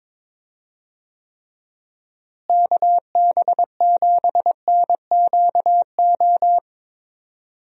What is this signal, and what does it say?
Morse code, heard as a keyed tone, about 22 wpm, 700 Hz. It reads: KB7NQO
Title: USA Call Signs - 6 Characters - Repeated 3x - 22wpm